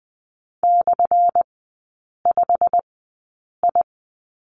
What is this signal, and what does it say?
Morse code, heard as a keyed tone, about 20 wpm, spaced as for 8 wpm, 690 Hz. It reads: /5I
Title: 212 Lesson 34e - Number 8 - Three Characters 20wpm